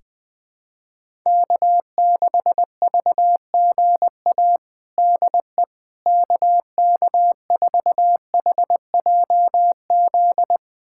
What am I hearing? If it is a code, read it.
Morse code, heard as a keyed tone, about 21 wpm, 700 Hz. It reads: K6VGA DE KK4HJZ
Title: USA Callsign DE Callsign 20wpm